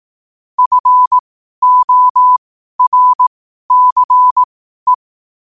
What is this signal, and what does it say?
Morse code, heard as a keyed tone, about 18 wpm, spaced as for 12 wpm, 1000 Hz. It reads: FORCE
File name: qotdfiles / 12wpm